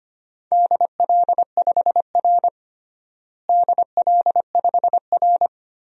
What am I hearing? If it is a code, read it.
Morse code, heard as a keyed tone, about 25 wpm, 700 Hz. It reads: DL5R DL5R
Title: DXCC-100 Call Signs - 4 Characters - Repeated 3x - 25wpm